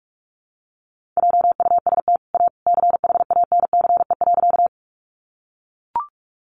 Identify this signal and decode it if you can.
Morse code, heard as a keyed tone, about 44 wpm, 700 Hz. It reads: JUST A CHANCE.
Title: Sentences from Top 800 Words 45wpm